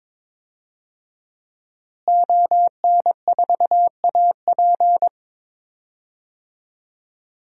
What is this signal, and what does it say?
Morse code, heard as a keyed tone, about 22 wpm, 700 Hz. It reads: ON4AP
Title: DXCC-100 Call Signs - 5 Characters - Repeated 2x - 22wpm